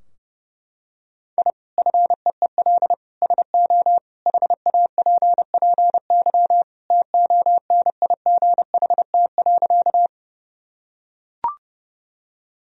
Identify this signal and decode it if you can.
Morse code, heard as a keyed tone, about 32 wpm, 700 Hz. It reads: I FEEL SO HAPPY TONIGHT.
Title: Sentences from Top 1000 Words - Review Part 3 - 30wpm